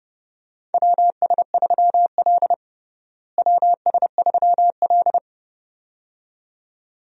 Morse code, heard as a keyed tone, about 30 wpm, 700 Hz. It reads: WS3L WS3L
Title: USA Call Signs - 4 Characters - Repeated 2x - 30wpm